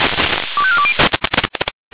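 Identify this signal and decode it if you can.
Five-tone selective call (ZVEI), heard as quick runs of five tones, tones of about 70 ms. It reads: 25420
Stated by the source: Five-tone ZVEI selective call received on SDR.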